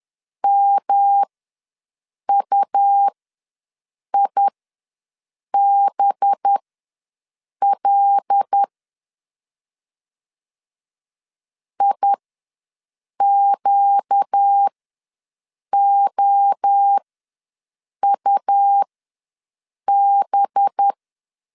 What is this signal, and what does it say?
Morse code, heard as a keyed tone, about 10 wpm, spaced as for 6 wpm, 800 Hz. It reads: MUIBL IQOUB
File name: LEZIONI / 5 LBVC / 5 LB ws 10wpm 3616